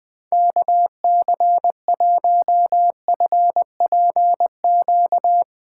Morse code, heard as a keyed tone, about 20 wpm, 700 Hz. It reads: KC1FPQ
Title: USA Callsign DE Callsign 20wpm